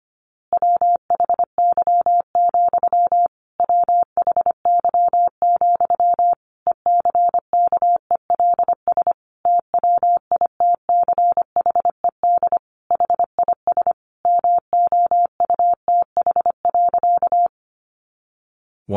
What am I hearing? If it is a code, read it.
Morse code, heard as a keyed tone, about 25 wpm, 700 Hz. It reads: WHY, WHY, ECKELS TWITCHED HIS MOUTH.
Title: A Sound Of Thunder 25wpm